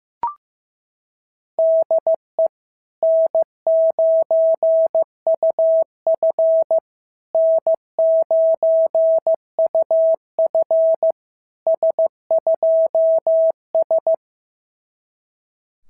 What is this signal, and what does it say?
Morse code, heard as a keyed tone, about 15 wpm, 650 Hz. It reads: DE N9UF N9UF S2S
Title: SOTA QSO Practice 15wpm